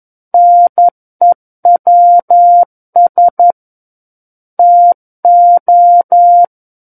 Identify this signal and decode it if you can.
Morse code, heard as a keyed tone, about 11 wpm, 700 Hz. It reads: NEWS TO
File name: Book One / 02-Chapter 2 - 11 WPM